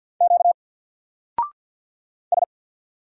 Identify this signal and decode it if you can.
Morse code, heard as a keyed tone, about 48 wpm, 700 Hz. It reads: Q S
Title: Single Letter-Number 50wpm